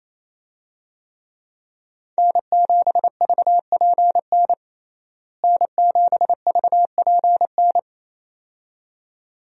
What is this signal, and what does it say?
Morse code, heard as a keyed tone, about 28 wpm, 700 Hz. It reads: N7VPN N7VPN
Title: USA Call Signs - 5 Characters - Repeated 2x - 28wpm